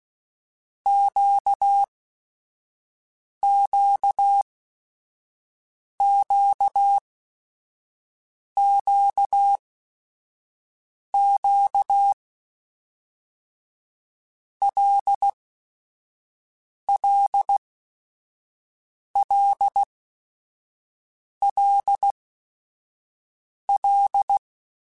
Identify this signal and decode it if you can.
Morse code, heard as a keyed tone, about 16 wpm, spaced as for 5 wpm, 800 Hz. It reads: QQQQQ LLLLL
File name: Basic Sets / Basic Letter Groups Collective7 / LFCYQ@5